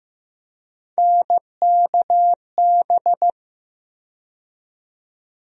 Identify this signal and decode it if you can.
Morse code, heard as a keyed tone, about 15 wpm, 700 Hz. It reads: NKB
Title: USA Call Sign Postfixes - 3 Letters 15wpm